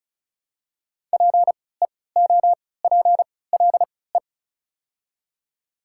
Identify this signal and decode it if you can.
Morse code, heard as a keyed tone, about 34 wpm, spaced as for 20 wpm, 700 Hz. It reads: PEOPLE
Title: 3x Character Spacing - Top 100 Words - 35wpm